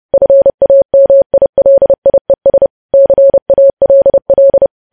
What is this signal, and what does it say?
Morse code, heard as a keyed tone, about 30 wpm, 550 Hz. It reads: FAMILIES CALL
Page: news headlines in morse code at 30 wpm